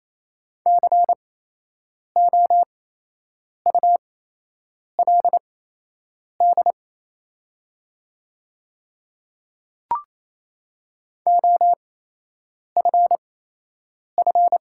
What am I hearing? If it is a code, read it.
Morse code, heard as a keyed tone, about 27 wpm, spaced as for 8 wpm, 690 Hz. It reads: COULD OFF
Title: 8x Character Spacing - Top 300 Words - 28wpm